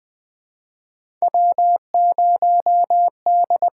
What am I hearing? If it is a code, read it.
Morse code, heard as a keyed tone, about 21 wpm, 700 Hz. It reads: W0D
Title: USA Call Signs - 5 Characters - 6x Character Spacing - 20wpm